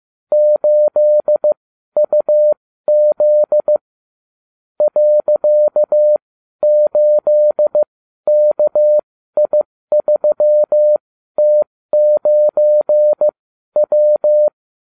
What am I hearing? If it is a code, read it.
Morse code, heard as a keyed tone, about 15 wpm, spaced as for 12 wpm, 600 Hz. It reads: 8UZ .8KI3T9W
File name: cw / ubacw30 3